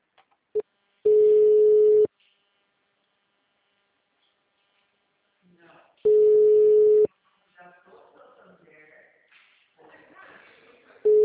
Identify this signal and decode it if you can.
Ringback tone (Europe, 425 Hz), with speech.
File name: out- -2009- - - .86758